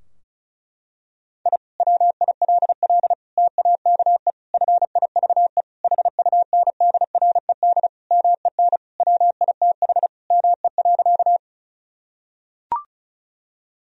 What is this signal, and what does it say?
Morse code, heard as a keyed tone, about 34 wpm, 700 Hz. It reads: I WILL TAKE FIVE HUNDRED MEN WITH ME.
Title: Sentences from Top 1000 Words - Review Part 2 - 35wpm